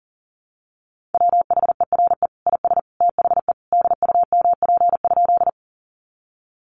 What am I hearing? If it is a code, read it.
Morse code, heard as a keyed tone, about 40 wpm, 700 Hz. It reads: WHERE IS THE DUMP?
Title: Sentences from Top 1300 Words 40wpm